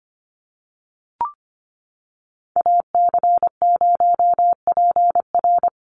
Morse code, heard as a keyed tone, about 25 wpm, 690 Hz. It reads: AC0PR
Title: NA SOTA Activator Callsigns 25wpm